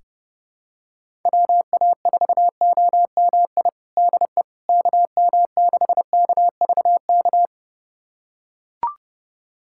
Morse code, heard as a keyed tone, about 28 wpm, 700 Hz. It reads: WA4OMI DE KM6KVK
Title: USA Callsign DE Callsign - 6 Characters - 30wpm